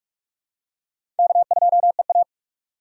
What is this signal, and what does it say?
Morse code, heard as a keyed tone, about 45 wpm, 700 Hz. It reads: K2EA
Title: USA Call Signs - 4 Characters - 2x Character Spacing - 45wpm